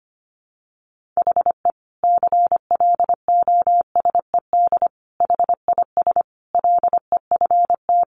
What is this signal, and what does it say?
Morse code, heard as a keyed tone, about 25 wpm, 700 Hz. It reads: HE CLOSED HIS LEFT
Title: Sentences from Top 1400 Words 25wpm